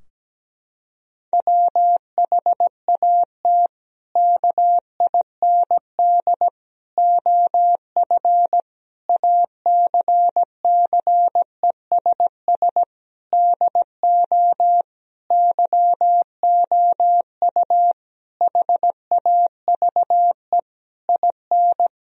Morse code, heard as a keyed tone, about 17 wpm, 700 Hz. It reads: WHAT KIND OF ACCESS DO YOU HAVE IN
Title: Sentences from Top 1000 Words - Review Part 4 - 17wpm